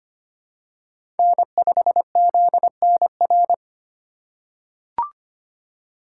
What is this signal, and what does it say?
Morse code, heard as a keyed tone, about 25 wpm, 690 Hz. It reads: N5ZNR
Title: USA Call Signs - 4x Character Spacing - 25wpm